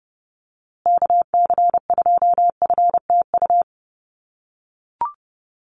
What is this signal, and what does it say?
Morse code, heard as a keyed tone, about 30 wpm, 700 Hz. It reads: KC2FTU
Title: USA Call Signs Repeated 2x 30wpm